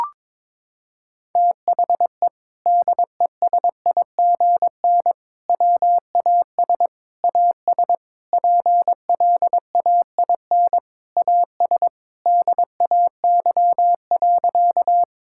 Morse code, heard as a keyed tone, about 22 wpm, 700 Hz. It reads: THE DESIGN WAS AS PLAIN AS DAY.
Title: Sentences from Top 1000 Words - Review Part 5 - 22wpm